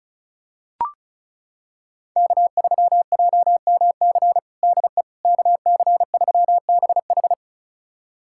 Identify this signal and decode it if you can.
Morse code, heard as a keyed tone, about 36 wpm, 700 Hz. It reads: K3JMC DE KC3BH
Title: USA Callsign DE Callsign - 5 Characters - 35wpm